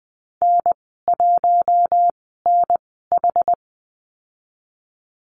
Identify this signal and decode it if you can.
Morse code, heard as a keyed tone, about 20 wpm, spaced as for 14 wpm, 700 Hz. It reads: N1NH
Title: 082 Lesson 14g - Number 2 - Call Signs 20wpm